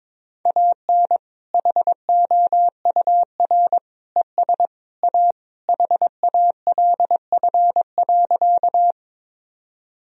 Morse code, heard as a keyed tone, about 22 wpm, 700 Hz. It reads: AN HOUR ES A HALF.
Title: Sentences from Top 400 Words and Abbreviations 22wpm